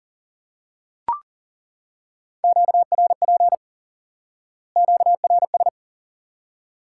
Morse code, heard as a keyed tone, about 40 wpm, 700 Hz. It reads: QRP QRS
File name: IQE Course 3 - Lesson 89 - YAGI - Two Elements - 40wpm